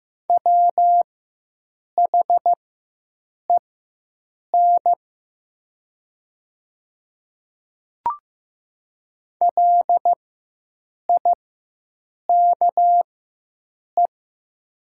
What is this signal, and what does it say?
Morse code, heard as a keyed tone, about 15 wpm, spaced as for 7 wpm, 700 Hz. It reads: WHEN LIKE
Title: 4x Character Spacing - Top 100 Words - 15wpm